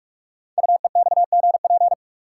Morse code, heard as a keyed tone, about 44 wpm, 700 Hz. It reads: UEXGP
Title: Code Groups - 5 Letters - 45wpm